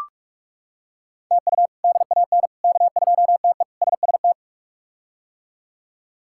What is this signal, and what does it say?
Morse code, heard as a keyed tone, about 42 wpm, 700 Hz. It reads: TU DAN K2TE SST
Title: K1USN SST QSO Practice - 45wpm